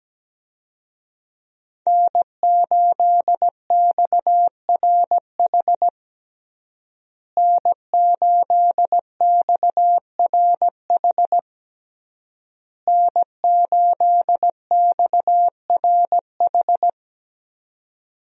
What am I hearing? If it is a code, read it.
Morse code, heard as a keyed tone, about 17 wpm, 700 Hz. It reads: N8XRH N8XRH N8XRH
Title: USA Call Signs - 5 Characters - Repeated 3x - 17wpm